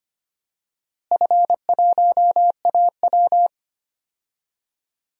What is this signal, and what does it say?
Morse code, heard as a keyed tone, about 25 wpm, 700 Hz. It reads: F1AW
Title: DXCC-100 Call Signs - 4 Characters - Repeated 2x - 25wpm